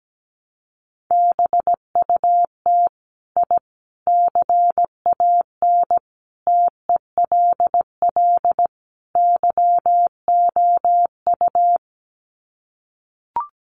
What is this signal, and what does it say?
Morse code, heard as a keyed tone, about 17 wpm, 700 Hz. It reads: BUT I CAN TELL YOU
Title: Sets of 5 Words Encore 17wpm